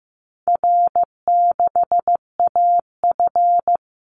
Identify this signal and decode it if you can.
Morse code, heard as a keyed tone, about 15 wpm, 700 Hz. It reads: R6AF
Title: Easy Call Signs Repeated 3x - 15wpm